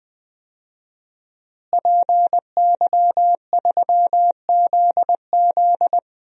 Morse code, heard as a keyed tone, about 20 wpm, 700 Hz. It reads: PY3ZZ
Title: DXCC-100 Call Signs - 5 Characters - Repeated 2x - 20wpm